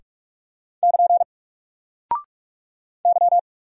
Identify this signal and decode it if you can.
Morse code, heard as a keyed tone, about 44 wpm, 700 Hz. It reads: <KN> Y
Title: Single Character (Letter:Number:Punctuation:Prosign) 45wpm